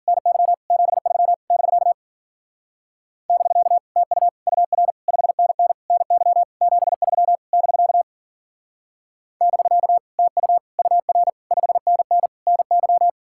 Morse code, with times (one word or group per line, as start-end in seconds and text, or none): 0.1-0.5 NY
0.7-1.3 73
1.5-1.9 <BK>
3.3-3.8 <BK>
4.0-4.3 TU
4.5-4.9 UR
5.1-5.7 5NN
5.9-6.4 NY
6.6-7.4 73
7.5-8.0 <BK>
9.4-10.0 <BK>
10.2-10.6 TU
10.8-11.3 UR
11.5-12.3 5NN
12.5-13.1 NY